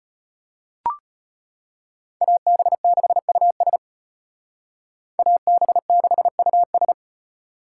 Morse code, heard as a keyed tone, about 32 wpm, 700 Hz. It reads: AB6US AB6US